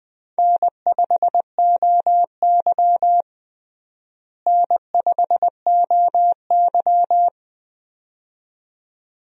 Morse code, heard as a keyed tone, about 20 wpm, 700 Hz. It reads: N5OY N5OY
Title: USA Call Signs - 4 Characters - Repeated 2x - 20wpm